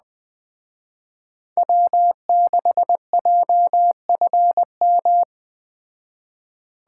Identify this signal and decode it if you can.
Morse code, heard as a keyed tone, about 20 wpm, 700 Hz. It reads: W6JFM